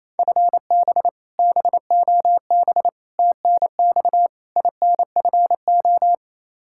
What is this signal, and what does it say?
Morse code, heard as a keyed tone, about 28 wpm, 700 Hz. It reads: FB BOB TNX INFO
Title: K1USN SST QSO Practice 28wpm